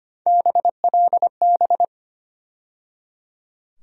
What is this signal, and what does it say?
Morse code, heard as a keyed tone, about 25 wpm, 700 Hz. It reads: BLB
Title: USA Call Sign Postfixes - 3 Letters 25wpm